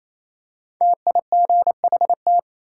Morse code, heard as a keyed tone, about 28 wpm, 700 Hz. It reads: TIGHT
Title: Five Letter Words - 28wpm